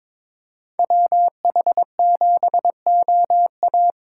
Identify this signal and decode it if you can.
Morse code, heard as a keyed tone, about 26 wpm, 700 Hz. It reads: WH7OA
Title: Speed Racing - USA 2x2 Call Signs 22wpm